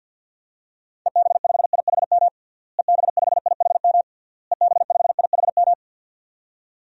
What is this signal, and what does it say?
Morse code, heard as a keyed tone, about 51 wpm, 700 Hz. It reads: EB5IHM EB5IHM EB5IHM
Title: DXCC-100 Call Signs - 6 Characters - Repeated 3x - 50wpm